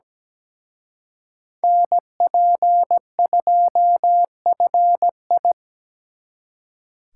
Morse code, heard as a keyed tone, about 20 wpm, 700 Hz. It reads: NP2FI
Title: Speed Racing - USA 2x2 Call Signs 17wpm